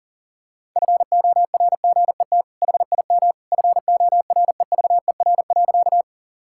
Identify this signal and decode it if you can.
Morse code, heard as a keyed tone, about 40 wpm, 700 Hz. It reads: FORGET HIM FOREVER.
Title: Sentences from Top 1600 Words 40wpm